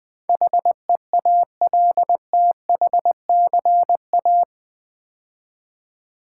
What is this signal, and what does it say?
Morse code, heard as a keyed tone, about 20 wpm, 700 Hz. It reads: HEALTHCA
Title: Build A Word - 4 Letters At A Time - 10-Letter Words - 20wpm